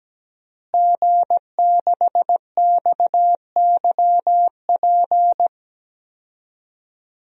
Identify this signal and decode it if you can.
Morse code, heard as a keyed tone, about 17 wpm, 700 Hz. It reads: G6XYP
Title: DXCC-100 Call Signs - 5 Characters - Repeated 3x - 17wpm